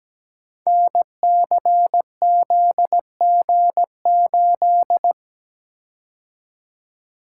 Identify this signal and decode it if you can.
Morse code, heard as a keyed tone, about 17 wpm, 700 Hz. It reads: NCZG8
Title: Code Groups - 5 Letter:Numbers - 17wpm